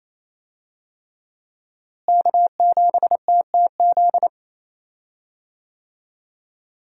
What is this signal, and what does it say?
Morse code, heard as a keyed tone, about 28 wpm, 700 Hz. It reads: K7TTZ
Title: USA Call Signs - 5 Characters - Repeated 3x - 28wpm